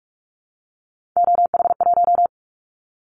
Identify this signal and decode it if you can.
Morse code, heard as a keyed tone, about 46 wpm, 700 Hz. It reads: OH1
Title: Callsign Prefixes for Top 50 DXCC - 45wpm